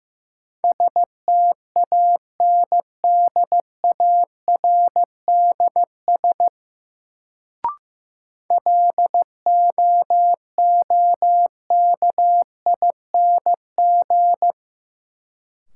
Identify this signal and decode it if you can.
Morse code, heard as a keyed tone, about 16 wpm, 700 Hz. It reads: STANDARDS LOOKING
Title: Sentences from Top 1000 Words - Review Part 6 - New Word Drill - 15wpm